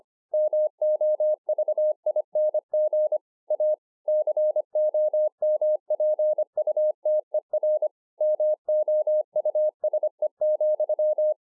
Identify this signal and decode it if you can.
Morse code, heard as a keyed tone, about 25 wpm, 610 Hz. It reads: MOVING A COMPUTER MOUSE,